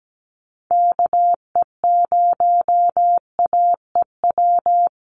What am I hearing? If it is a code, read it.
Morse code, heard as a keyed tone, about 17 wpm, 700 Hz. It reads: KE0AEW
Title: USA Call Signs 17wpm